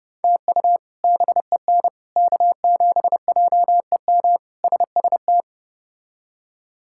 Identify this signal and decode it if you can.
Morse code, heard as a keyed tone, about 32 wpm, 700 Hz. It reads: TU BEN K7JEM SST
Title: K1USN SST QSO Practice - 30wpm